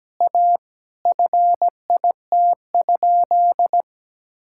Morse code, heard as a keyed tone, about 17 wpm, 700 Hz. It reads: A FIT?
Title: Sentences from Top 1000 Words - Review Part 5 - 17wpm